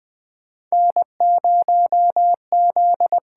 Morse code, heard as a keyed tone, about 20 wpm, 700 Hz. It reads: N0Z
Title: USA Call Signs - 5 Characters - 8x Character Spacing - 20wpm